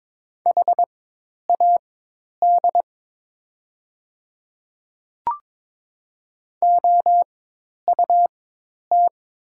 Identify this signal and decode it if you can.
Morse code, heard as a keyed tone, about 22 wpm, spaced as for 10 wpm, 700 Hz. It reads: HAD OUT
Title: 4x Character Spacing - Top 100 Words - 22wpm